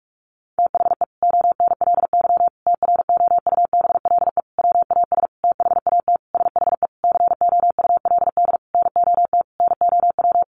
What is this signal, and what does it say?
Morse code, heard as a keyed tone, about 44 wpm, 700 Hz. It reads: THE ONLY TROUBLE WAS THAT SHE COULD NOT NOW
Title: Sentences from Top 1200 Words 45wpm